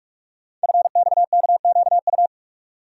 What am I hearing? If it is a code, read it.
Morse code, heard as a keyed tone, about 44 wpm, 700 Hz. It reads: FXKQU
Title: Code Groups - 5 Letters - 45wpm